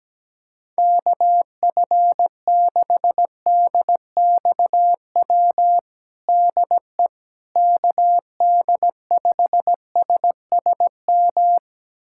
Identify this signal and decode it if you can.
Morse code, heard as a keyed tone, about 17 wpm, 700 Hz. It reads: KF6DXW DE KD5SSM
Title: USA Callsign DE Callsign - 6 Characters - 17wpm